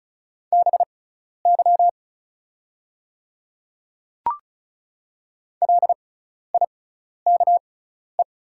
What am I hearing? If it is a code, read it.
Morse code, heard as a keyed tone, about 36 wpm, spaced as for 12 wpm, 700 Hz. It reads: BY LIKE
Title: 6x Character Spacing - Top 100 Words - 35wpm